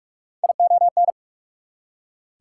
Morse code, heard as a keyed tone, about 60 wpm, spaced as for 53 wpm, 690 Hz. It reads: ION
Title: Speed-Racing Prefixes and Suffixes 45wpm